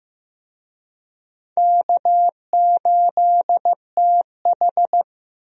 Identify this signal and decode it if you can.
Morse code, heard as a keyed tone, about 15 wpm, 700 Hz. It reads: K8TH
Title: USA Call Signs - 4 Characters - Repeated 3x - 15wpm